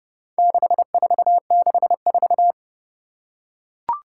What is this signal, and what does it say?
Morse code, heard as a keyed tone, about 31 wpm, 700 Hz. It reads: 6464
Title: Four Challenging Letter-Number Combinations 30wpm